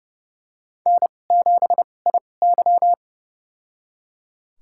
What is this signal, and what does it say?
Morse code, heard as a keyed tone, about 31 wpm, spaced as for 22 wpm, 700 Hz. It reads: N7IY
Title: USA Call Signs - 2x Character Spacing - 30wpm